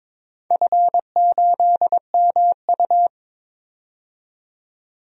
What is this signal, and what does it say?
Morse code, heard as a keyed tone, about 22 wpm, 700 Hz. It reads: F8MU
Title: DXCC-100 Call Signs - 4 Characters - 22wpm